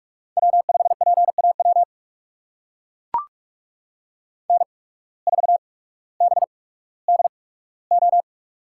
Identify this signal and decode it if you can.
Morse code, heard as a keyed tone, about 44 wpm, spaced as for 19 wpm, 700 Hz. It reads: W5PAW N4BDO
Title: USA Call Signs - 5 Characters - 8x Character Spacing - 45wpm